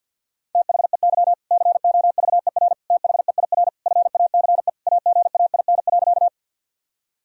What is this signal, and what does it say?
Morse code, heard as a keyed tone, about 51 wpm, 700 Hz. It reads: THEY COVER THEIR FACE AGAIN.
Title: Sentences from Top 1000 Words - Review Part 1 - 50wpm